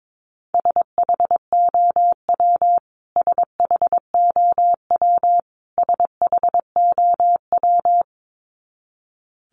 Morse code, heard as a keyed tone, about 22 wpm, 700 Hz. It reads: SHOW SHOW SHOW
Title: Top 500 Words Repeated 22wpm